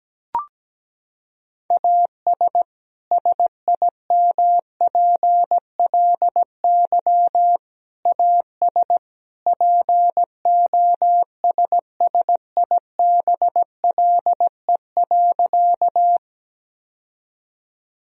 Morse code, heard as a keyed tone, about 17 wpm, 700 Hz. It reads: AS SIMPLY AS POSSIBLE.
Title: Sentences from Top 1000 Words - Review Part 4 - 17wpm